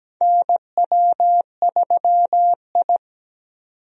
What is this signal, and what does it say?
Morse code, heard as a keyed tone, about 17 wpm, 700 Hz. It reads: NW3I
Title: USA Call Signs - 4 Characters - 17wpm